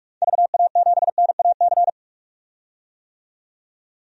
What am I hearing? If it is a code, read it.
Morse code, heard as a keyed tone, about 43 wpm, 700 Hz. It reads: VA7NAC
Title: DXCC-100 Call Signs - 6 Characters - 45wpm